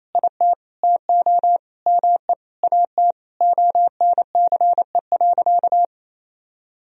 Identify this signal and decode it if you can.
Morse code, heard as a keyed tone, about 28 wpm, 700 Hz. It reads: IT TO ME AT ONCE.
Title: Sentences from Top 500 Words and Abbreviations 28wpm